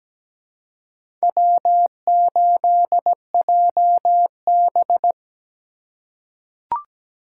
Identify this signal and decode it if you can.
Morse code, heard as a keyed tone, about 17 wpm, 700 Hz. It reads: W8JB
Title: USA Call Signs - 4 Characters - 4x Character Spacing - 17wpm